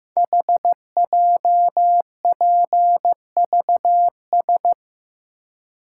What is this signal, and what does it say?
Morse code, heard as a keyed tone, about 15 wpm, 700 Hz. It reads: HJPVS
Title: Code Groups - 5 Letters - 15wpm